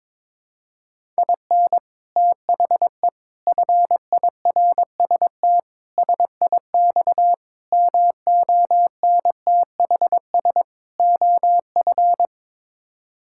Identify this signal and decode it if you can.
Morse code, heard as a keyed tone, about 22 wpm, 700 Hz. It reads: IN THE FIRST SIX MONTHS OF
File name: practice / 22wpm / Sets of 6 Words - 22wpm